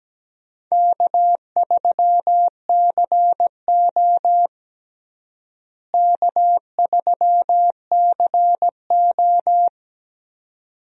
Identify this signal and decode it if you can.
Morse code, heard as a keyed tone, about 17 wpm, 700 Hz. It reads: K3CO K3CO
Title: USA Call Signs - 4 Characters - Repeated 2x - 17wpm